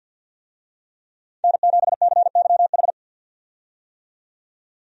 Morse code, heard as a keyed tone, about 49 wpm, 700 Hz. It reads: N7CYH
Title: USA Call Signs - 5 Characters - 50wpm